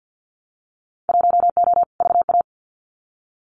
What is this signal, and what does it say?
Morse code, heard as a keyed tone, about 50 wpm, 700 Hz. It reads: 1O VA
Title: Winter Field Day Exchanges 50wpm